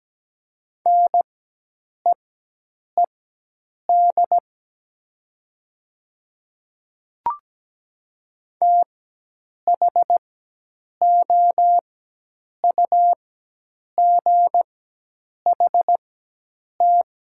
Morse code, heard as a keyed tone, about 17 wpm, spaced as for 8 wpm, 700 Hz. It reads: NEED THOUGHT
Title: 4x Character Spacing - Top 300 Words - 17wpm